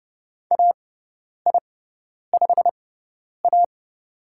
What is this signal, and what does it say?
Morse code, heard as a keyed tone, about 30 wpm, spaced as for 10 wpm, 700 Hz. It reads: AI5A
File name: Lesson 17g - Letter u - Call Signs 30wpm